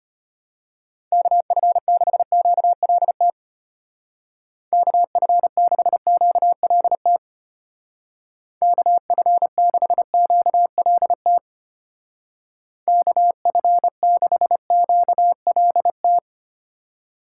Morse code, heard as a keyed tone, about 31 wpm, 700 Hz. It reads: KF6QLT KF6QLT KF6QLT KF6QLT
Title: Speed Racing - USA Call Signs 25wpm